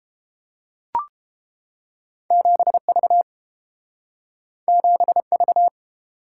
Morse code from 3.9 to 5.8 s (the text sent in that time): 7V